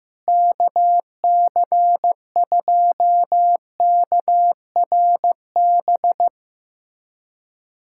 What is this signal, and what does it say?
Morse code, heard as a keyed tone, about 15 wpm, 700 Hz. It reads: KC2KRB
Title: USA Callsign DE Callsign - 6 Characters - 15wpm